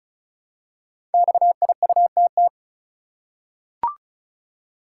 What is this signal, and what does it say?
Morse code, heard as a keyed tone, about 36 wpm, 700 Hz. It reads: XIUTT
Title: Code Groups - 5 Letters - 35wpm